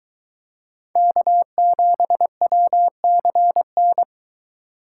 Morse code, heard as a keyed tone, about 24 wpm, 700 Hz. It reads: K7WCN